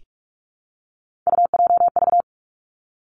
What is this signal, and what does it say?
Morse code, heard as a keyed tone, about 46 wpm, 700 Hz. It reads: UJV
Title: Three Challenging Letter-Number Combinations 45wpm